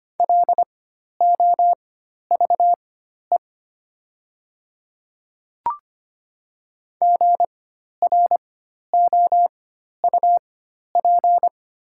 Morse code, heard as a keyed tone, about 25 wpm, spaced as for 12 wpm, 700 Hz. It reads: LOVE GROUP
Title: 4x Character Spacing - Top 500 Words - 25wpm